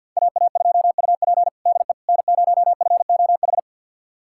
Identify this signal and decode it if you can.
Morse code, heard as a keyed tone, about 49 wpm, 700 Hz. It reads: AA2UP DE N0FOH
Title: USA Callsign DE Callsign - 5 Characters - 50wpm